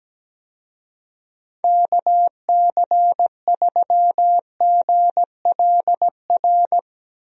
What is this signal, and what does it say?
Morse code, heard as a keyed tone, about 17 wpm, 700 Hz. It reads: KC3GLR
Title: USA Call Signs - 6 Characters - Repeated 3x - 17wpm